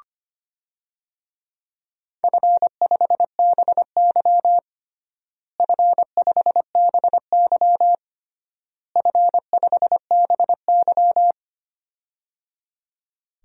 Morse code, heard as a keyed tone, about 25 wpm, 700 Hz. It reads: F5BY F5BY F5BY
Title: DXCC-100 Call Signs - 4 Characters - Repeated 3x - 25wpm